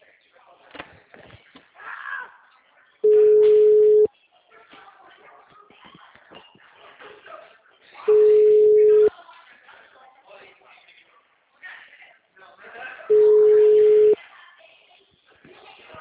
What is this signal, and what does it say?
Ringback tone (Europe, 425 Hz), with speech.